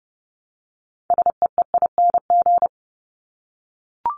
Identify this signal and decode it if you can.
Morse code, heard as a keyed tone, about 30 wpm, 700 Hz. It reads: SEEING
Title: ING Suffix - Top 100 Words with Base Words 30wpm